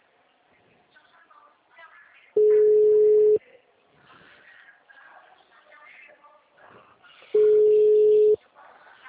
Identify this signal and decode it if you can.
Ringback tone (Europe, 425 Hz), with speech.